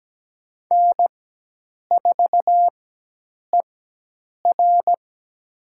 Morse code, heard as a keyed tone, about 17 wpm, spaced as for 8 wpm, 700 Hz. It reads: N4ER
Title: USA Call Signs - 4 Characters - 4x Character Spacing - 17wpm